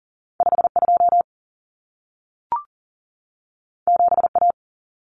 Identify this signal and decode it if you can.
Morse code, heard as a keyed tone, about 40 wpm, 700 Hz. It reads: 52 7A
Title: Common Cut Numbers - 2-Digit 40wpm